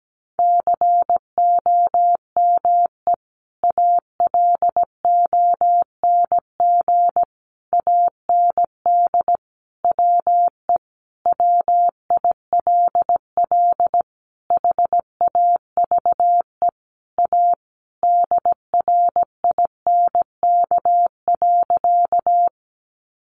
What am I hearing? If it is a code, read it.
Morse code, heard as a keyed tone, about 17 wpm, 700 Hz. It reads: COME ALONG AND WE WILL HAVE A DRINK.
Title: Sentences from Top 1500 Words 17wpm